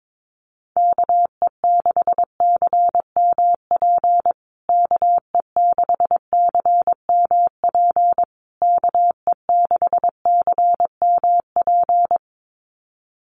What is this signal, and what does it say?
Morse code, heard as a keyed tone, about 22 wpm, 700 Hz. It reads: KE6CMP KE6CMP KE6CMP
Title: USA Call Signs Repeated 3x 22wpm